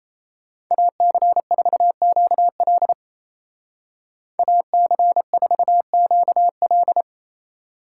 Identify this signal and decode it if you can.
Morse code, heard as a keyed tone, about 34 wpm, 700 Hz. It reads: AC4QL AC4QL
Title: Speed Racing - USA 2x2 Call Signs 28wpm